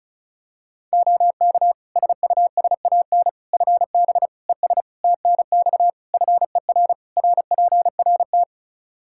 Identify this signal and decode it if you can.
Morse code, heard as a keyed tone, about 34 wpm, 700 Hz. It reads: OK SUSAN FB ES TNX FER RPRT
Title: Practice QSO - Exchange 2 - Info - 35wpm